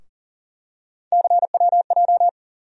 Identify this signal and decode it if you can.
Morse code, heard as a keyed tone, about 40 wpm, 700 Hz. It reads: CWJ
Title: USA Call Sign Postfixes - 3 Letters 40wpm